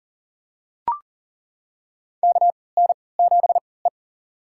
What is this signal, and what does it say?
Morse code, heard as a keyed tone, about 40 wpm, spaced as for 23 wpm, 700 Hz. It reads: KN7E
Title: USA Call Signs - 4 Characters - 3x Character Spacing - 40wpm